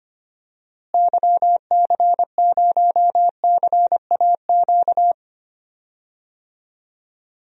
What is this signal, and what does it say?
Morse code, heard as a keyed tone, about 25 wpm, 700 Hz. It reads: YC0CAQ
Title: DXCC-100 Call Signs - 6 Characters - Repeated 2x - 25wpm